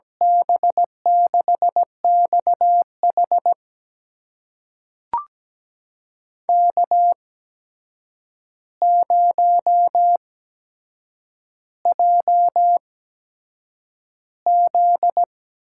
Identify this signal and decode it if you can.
Morse code, heard as a keyed tone, about 17 wpm, spaced as for 7 wpm, 700 Hz. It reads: B6XH K0JZ
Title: USA Call Signs - 8x Character Spacing - 17wpm